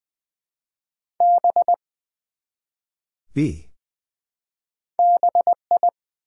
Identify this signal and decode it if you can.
Morse code, heard as a keyed tone, about 20 wpm, 700 Hz. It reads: B BI
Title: Build A Word - 1 Letter At A Time - 3-Letter Words - 20wpm